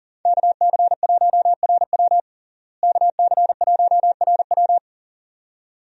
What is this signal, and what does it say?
Morse code, heard as a keyed tone, about 40 wpm, 700 Hz. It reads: KC1RW KC1RW
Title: USA Call Signs - 5 Characters - Repeated 2x - 40wpm